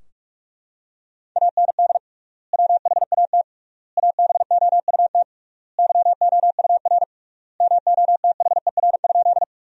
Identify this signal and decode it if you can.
Morse code, heard as a keyed tone, about 43 wpm, 700 Hz. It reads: AND WHAT ABOUT YOUR MOTHER?
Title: 3x Word-Spacing - Sentences from Top 200 Words - 45wpm